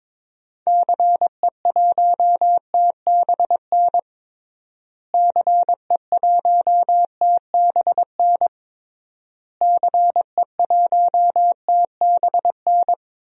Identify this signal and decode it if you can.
Morse code, heard as a keyed tone, about 22 wpm, 700 Hz. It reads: CE1TBN CE1TBN CE1TBN
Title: DXCC-100 Call Signs - 6 Characters - Repeated 3x - 22wpm